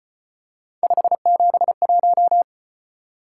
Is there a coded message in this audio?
Yes: Morse code from 0.8 to 2.4 s, heard as a keyed tone, about 33 wpm, 700 Hz. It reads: 571